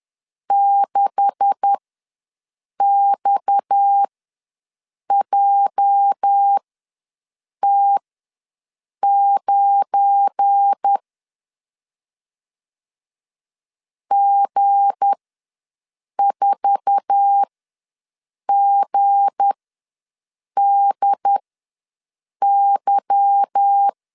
Morse code, heard as a keyed tone, about 10 wpm, spaced as for 6 wpm, 800 Hz. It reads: 6XJT9 G4GDY
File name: LEZIONI / 9 286 ws 10wpm 4032